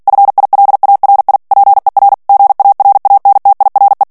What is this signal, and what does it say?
Morse code, heard as a keyed tone, about 48 wpm, 800 Hz. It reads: FILARI PER GARANTIRE
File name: morse mp3 / ITALIEN texte clair 1 vitesse 45 mots minute